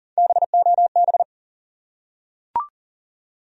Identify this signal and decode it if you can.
Morse code, heard as a keyed tone, about 39 wpm, 700 Hz. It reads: BOB
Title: USA Common Names - Max 3 Letters - 40wpm